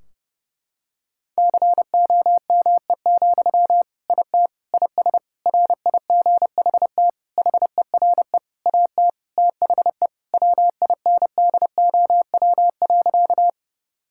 Morse code, heard as a keyed tone, about 32 wpm, 700 Hz. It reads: COME, IT IS RIGHT HERE AT THE WINDOW.
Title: Sentences from Top 1600 Words 30wpm